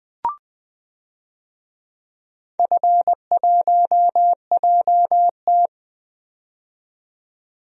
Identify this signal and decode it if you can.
Morse code, heard as a keyed tone, about 20 wpm, 700 Hz. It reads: F1JT
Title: DXCC-100 Call Signs - 4 Characters - 20wpm